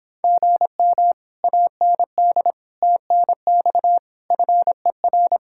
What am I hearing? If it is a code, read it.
Morse code, heard as a keyed tone, about 26 wpm, 700 Hz. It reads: GM AND TNX FER